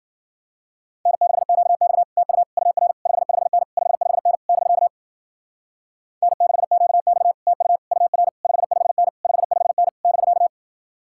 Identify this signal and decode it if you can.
Morse code, heard as a keyed tone, about 52 wpm, 700 Hz. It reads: N6QX TU UR 55N 55N <BK> N6QX TU UR 55N 55N <BK>